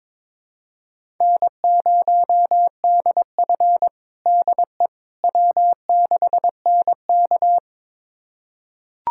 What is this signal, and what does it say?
Morse code, heard as a keyed tone, about 22 wpm, 700 Hz. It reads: N0DF DE W6NK
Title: USA Callsign DE Callsign - 4 Characters - 22wpm